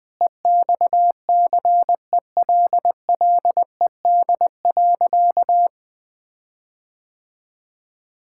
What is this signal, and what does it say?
Morse code, heard as a keyed tone, about 21 wpm, 700 Hz. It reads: EXCELLED.
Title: Sentences with 6-Word Phrases - 20wpm